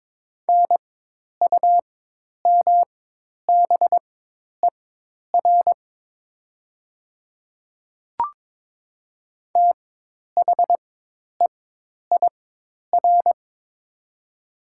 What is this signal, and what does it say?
Morse code, heard as a keyed tone, about 22 wpm, spaced as for 10 wpm, 700 Hz. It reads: NUMBER THEIR
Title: 4x Character Spacing - Top 100 Words - 22wpm